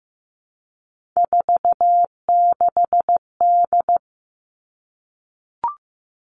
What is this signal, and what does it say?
Morse code, heard as a keyed tone, about 15 wpm, 700 Hz. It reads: 46D
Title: Three Challenging Letter-Number Combinations 15wpm